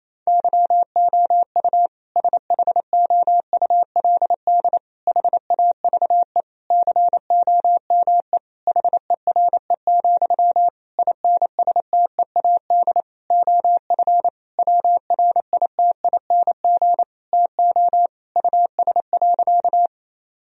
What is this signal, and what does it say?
Morse code, heard as a keyed tone, about 28 wpm, 700 Hz. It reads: YOU SHOULD HAVE COME HERE, INSTEAD OF WRITING TO US.
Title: Sentences from Top 1000 Words - Review Part 4 - 28wpm